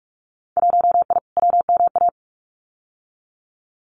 Morse code, heard as a keyed tone, about 46 wpm, 700 Hz. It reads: 1I WMA
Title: Winter Field Day Exchanges Encore 45wpm